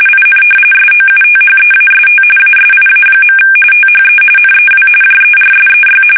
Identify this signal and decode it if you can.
RTTY, about 45.45 baud. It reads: IN SOUTHEAST ASIA. THE NEW JERSEY
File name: fsk850 / battleshipnj